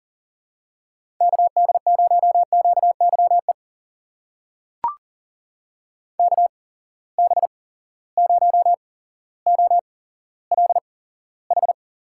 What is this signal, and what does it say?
Morse code, heard as a keyed tone, about 40 wpm, spaced as for 17 wpm, 700 Hz. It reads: KD0QYE KB0OLH
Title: USA Call Signs - 6 Characters - 8x Character Spacing - 40wpm